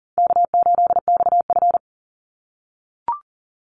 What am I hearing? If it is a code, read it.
Morse code, heard as a keyed tone, about 40 wpm, 700 Hz. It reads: K8XF
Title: Call Signs - 40wpm